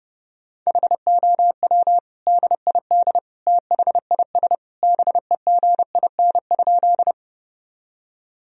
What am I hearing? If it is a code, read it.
Morse code, heard as a keyed tone, about 30 wpm, 700 Hz. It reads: HOW DID THIS BEGIN?
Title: Sentences from Top 1000 Words - Review Part 1 - 30wpm